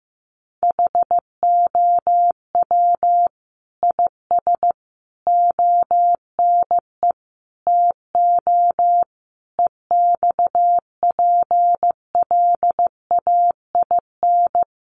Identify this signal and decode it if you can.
Morse code, heard as a keyed tone, about 15 wpm, 700 Hz. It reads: HOW IS ONE TO EXPLAIN
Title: Sentences from Top 1100 Words 15wpm